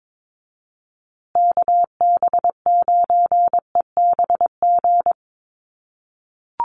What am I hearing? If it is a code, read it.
Morse code, heard as a keyed tone, about 22 wpm, 700 Hz. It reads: KB9EBG
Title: USA Call Signs Repeated 2x 22wpm